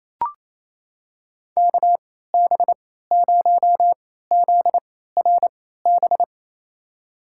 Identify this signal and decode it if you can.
Morse code, heard as a keyed tone, about 27 wpm, spaced as for 16 wpm, 700 Hz. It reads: KB0ZRB
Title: USA Call Signs - 6 Characters - 3x Character Spacing - 28wpm